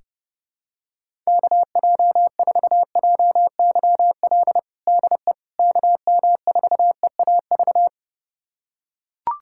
Morse code, heard as a keyed tone, about 30 wpm, 700 Hz. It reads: KJ4JYL DE KM4EAV
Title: USA Callsign DE Callsign 30wpm